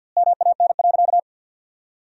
Morse code, heard as a keyed tone, about 53 wpm, 700 Hz. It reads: MAN.
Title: 2x Word-Spacing - Sentences from Top 200 Words - 50wpm